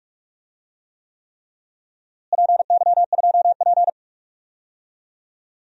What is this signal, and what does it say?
Morse code, heard as a keyed tone, about 41 wpm, 700 Hz. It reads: PY2P
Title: DXCC-100 Call Signs - 4 Characters - Repeated 3x - 45wpm